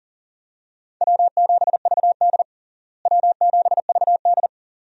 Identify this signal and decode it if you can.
Morse code, heard as a keyed tone, about 40 wpm, 700 Hz. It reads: W7VD W7VD
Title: USA Call Signs - 4 Characters - Repeated 2x - 40wpm